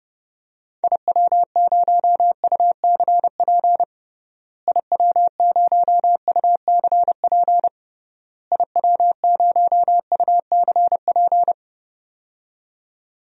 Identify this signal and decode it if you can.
Morse code, heard as a keyed tone, about 32 wpm, 700 Hz. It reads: IW0UCP IW0UCP IW0UCP
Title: DXCC-100 Call Signs - 6 Characters - Repeated 3x - 30wpm